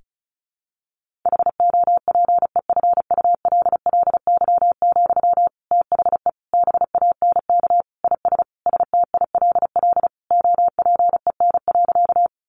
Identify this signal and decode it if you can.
Morse code, heard as a keyed tone, about 34 wpm, 700 Hz. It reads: HOPEFULLY, THE BANK IS STILL OPEN.
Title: Sentences from Top 1500 Words 35wpm